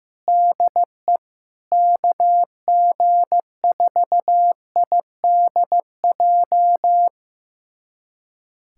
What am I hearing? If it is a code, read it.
Morse code, heard as a keyed tone, about 16 wpm, 700 Hz. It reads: DE KG4IDJ
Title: USA Callsign DE Callsign 15wpm